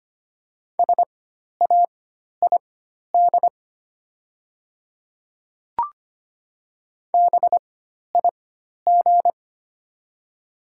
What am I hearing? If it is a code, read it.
Morse code, heard as a keyed tone, about 25 wpm, spaced as for 12 wpm, 690 Hz. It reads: SAID BIG
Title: 4x Character Spacing - Top 200 Words - 25wpm